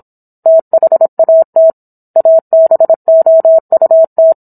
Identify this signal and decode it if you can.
Morse code, heard as a keyed tone, about 27 wpm, 650 Hz. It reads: THAT ABOUT